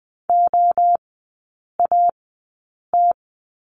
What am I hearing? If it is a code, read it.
Morse code, heard as a keyed tone, about 20 wpm, spaced as for 8 wpm, 700 Hz. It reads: OAT
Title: 041 Lesson 8e - Number 1 - Three Characters 20wpm